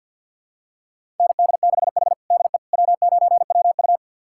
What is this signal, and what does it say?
Morse code, heard as a keyed tone, about 51 wpm, 700 Hz. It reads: ND6H DE W9WU
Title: USA Callsign DE Callsign - 4 Characters - 50wpm